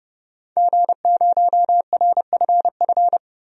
Morse code, heard as a keyed tone, about 31 wpm, 700 Hz. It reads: G0RFF
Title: DXCC-100 Call Signs - 5 Characters - 30wpm